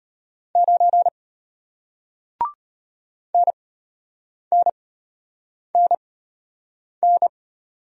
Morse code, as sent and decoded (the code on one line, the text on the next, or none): ----. / -. -. -. -.
9 NNNN